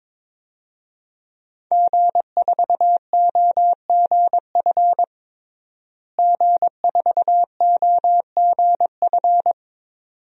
Morse code, heard as a keyed tone, about 22 wpm, 700 Hz. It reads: G4OGF G4OGF
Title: DXCC-100 Call Signs - 5 Characters - Repeated 2x - 22wpm